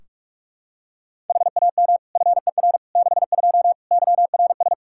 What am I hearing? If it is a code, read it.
Morse code, heard as a keyed tone, about 46 wpm, 700 Hz. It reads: HAM FER 62 YRS
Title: Practice QSO - Exchange 3 - More Info - 45wpm